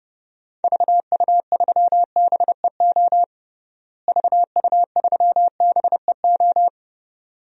Morse code, heard as a keyed tone, about 30 wpm, 700 Hz. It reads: VU3BEO VU3BEO
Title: DXCC-100 Call Signs - 6 Characters - Repeated 2x - 30wpm